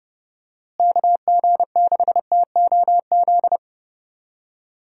Morse code, heard as a keyed tone, about 30 wpm, 700 Hz. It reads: KG6TOZ
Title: USA Call Signs - 6 Characters - Repeated 2x - 30wpm